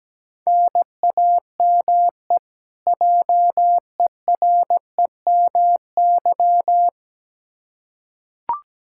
Morse code, heard as a keyed tone, about 17 wpm, 700 Hz. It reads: NAME JEREMY
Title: K1USN SST QSO Practice 17wpm